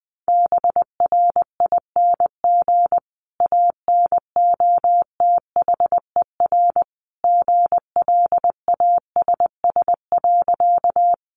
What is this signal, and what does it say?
Morse code, heard as a keyed tone, about 20 wpm, 700 Hz. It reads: BRING ANOTHER GLASS.
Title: Sentences from Top 1300 Words 20wpm